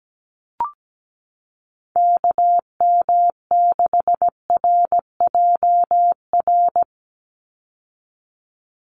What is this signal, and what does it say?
Morse code, heard as a keyed tone, about 17 wpm, 700 Hz. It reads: KM6RJR
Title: USA Call Signs 17wpm